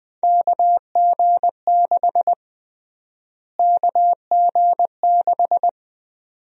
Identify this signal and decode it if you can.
Morse code, heard as a keyed tone, about 21 wpm, 700 Hz. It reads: KG6 KG6
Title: USA Call Sign Prefixes - Repeated 2x - 20wpm